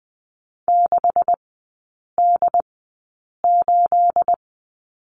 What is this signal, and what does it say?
Morse code, heard as a keyed tone, about 20 wpm, spaced as for 8 wpm, 700 Hz. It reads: 6D8
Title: 212 Lesson 34e - Number 8 - Three Characters 20wpm